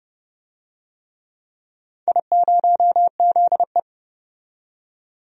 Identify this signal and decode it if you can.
Morse code, heard as a keyed tone, about 30 wpm, 700 Hz. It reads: I0ZE
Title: DXCC-100 Call Signs - 4 Characters - 30wpm